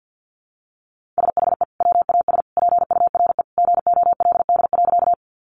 Morse code, heard as a keyed tone, about 50 wpm, 700 Hz. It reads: SHE WAS PURE GOLD.
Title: Sentences from Top 1500 Words 50wpm